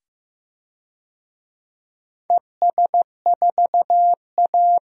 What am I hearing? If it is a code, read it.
Morse code, heard as a keyed tone, about 16 wpm, 700 Hz. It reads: ES4A
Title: DXCC-100 Call Signs - 4 Characters - 15wpm